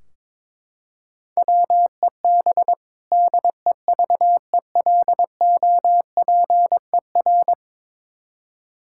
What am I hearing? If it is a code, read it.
Morse code, heard as a keyed tone, about 22 wpm, 700 Hz. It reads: WEB DEVELOPER
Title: Practice QSO - Exchange 3 - More Info - 22wpm